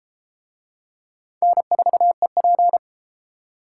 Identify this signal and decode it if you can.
Morse code, heard as a keyed tone, about 28 wpm, 700 Hz. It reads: N4EP